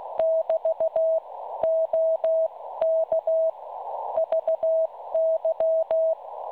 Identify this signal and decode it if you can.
Morse code, heard as a keyed tone, about 16 wpm, 650 Hz. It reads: <BT> OK VY
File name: cwronde / websdr recording .6kHz